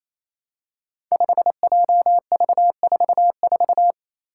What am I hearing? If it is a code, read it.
Morse code, heard as a keyed tone, about 28 wpm, 700 Hz. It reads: 5JV44
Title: Five Challenging Letter-Number Combinations 28wpm